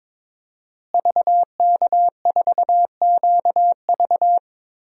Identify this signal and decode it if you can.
Morse code, heard as a keyed tone, about 22 wpm, 700 Hz. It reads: VK4QV
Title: DXCC-100 Call Signs - 5 Characters - Repeated 3x - 22wpm